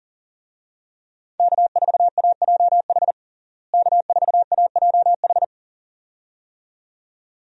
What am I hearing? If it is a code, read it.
Morse code, heard as a keyed tone, about 40 wpm, 700 Hz. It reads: K4AJH K4AJH
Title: USA Call Signs - 5 Characters - Repeated 2x - 40wpm